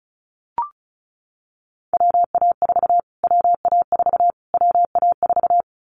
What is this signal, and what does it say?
Morse code, heard as a keyed tone, about 34 wpm, 700 Hz. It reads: WA4 WA4 WA4
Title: Callsign Prefixes for Top 25 DXCC - Repeated 3x - 35wpm